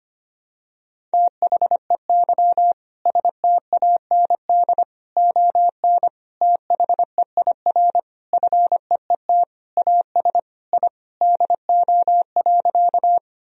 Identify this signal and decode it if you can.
Morse code, heard as a keyed tone, about 25 wpm, 700 Hz. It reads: THEY STAND ON THEIR FEET AS I DO.
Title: Sentences from Top 1000 Words - Review Part 1 - 25wpm